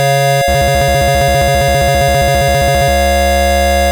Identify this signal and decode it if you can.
There is no code.